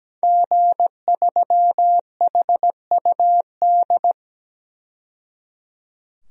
Morse code, heard as a keyed tone, about 17 wpm, 700 Hz. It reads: G3HUD
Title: DXCC-100 Call Signs - 5 Characters - 17wpm